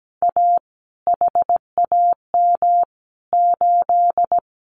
Morse code, heard as a keyed tone, about 17 wpm, 700 Hz. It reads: A HAM 8
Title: Common QSO Phrases Encore 17wpm